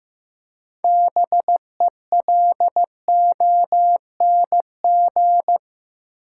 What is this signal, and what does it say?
Morse code, heard as a keyed tone, about 15 wpm, 700 Hz. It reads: BELONG
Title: Six Letter Words - 15wpm